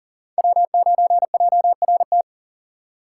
Morse code, heard as a keyed tone, about 40 wpm, 700 Hz. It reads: W9JRT
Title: USA Call Signs - 5 Characters - 2x Character Spacing - 40wpm